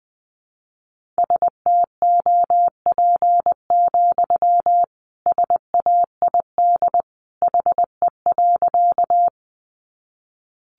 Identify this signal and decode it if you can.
Morse code, heard as a keyed tone, about 20 wpm, 700 Hz. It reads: STOP, SAID HE.
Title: Sentences from Top 300 Words 20wpm